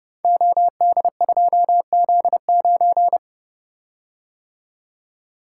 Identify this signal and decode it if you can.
Morse code, heard as a keyed tone, about 29 wpm, 700 Hz. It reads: OD2Z9
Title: Code Groups - 5 Letter:Numbers - 30wpm